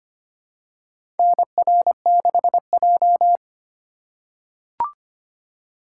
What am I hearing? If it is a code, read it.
Morse code, heard as a keyed tone, about 25 wpm, 700 Hz. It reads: NR6J
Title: USA Call Signs - 4 Characters - 4x Character Spacing - 25wpm